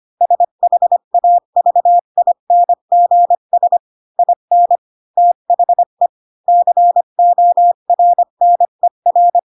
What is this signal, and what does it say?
Morse code, heard as a keyed tone, about 25 wpm, 700 Hz. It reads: SHAVINGS IN THE CORNER
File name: Pinocchio / Pinocchio ch7 20 25 ik1hge